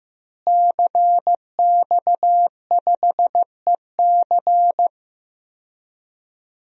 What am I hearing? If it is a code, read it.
Morse code, heard as a keyed tone, about 15 wpm, 700 Hz. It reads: CX5EC
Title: DXCC-100 Call Signs - 5 Characters - Repeated 3x - 15wpm